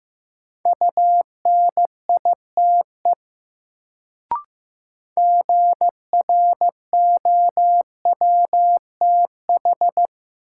Morse code, heard as a keyed tone, about 15 wpm, 700 Hz. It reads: UNITE GROWTH
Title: Sentences from Top 1000 Words - Review Part 5 - New Word Drill - 15wpm